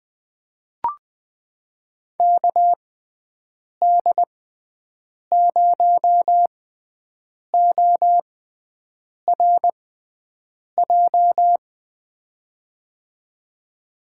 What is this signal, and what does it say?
Morse code, heard as a keyed tone, about 20 wpm, spaced as for 7 wpm, 700 Hz. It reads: KD0ORJ
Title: USA Call Signs - 6x Character Spacing - 20wpm